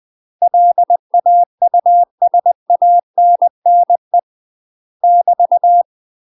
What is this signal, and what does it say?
Morse code, heard as a keyed tone, about 20 wpm, 700 Hz. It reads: LAUSANNE <BT>
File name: UK MASTER QSOs TRAINING / 20WPM / UK MASTER QSOs 0059